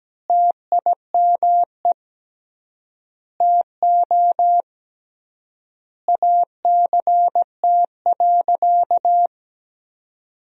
Morse code, heard as a keyed tone, about 17 wpm, 700 Hz. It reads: TIME TO ACT.
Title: 3x Word-Spacing - Sentences from Top 200 Words - 17wpm